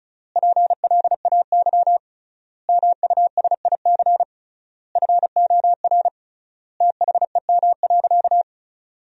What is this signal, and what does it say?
Morse code, heard as a keyed tone, about 36 wpm, 700 Hz. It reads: PLAY MUSIC FOR THEM.
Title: 3x Word-Spacing - Sentences from Top 300 Words - 35wpm